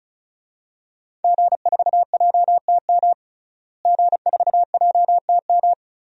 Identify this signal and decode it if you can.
Morse code, heard as a keyed tone, about 36 wpm, 700 Hz. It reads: G4JTM G4JTM
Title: DXCC-100 Call Signs - 5 Characters - Repeated 3x - 35wpm